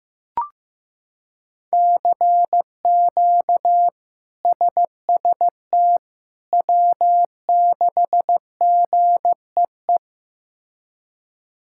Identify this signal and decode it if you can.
Morse code, heard as a keyed tone, about 15 wpm, 700 Hz. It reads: CQ SST W6GEE
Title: K1USN SST QSO Practice - 15wpm